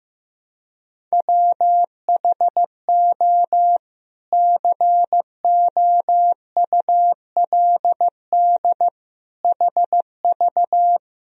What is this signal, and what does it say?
Morse code, heard as a keyed tone, about 15 wpm, 700 Hz. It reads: WHO COULD HV
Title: Sentences from Top 300 Words and Abbreviations 15wpm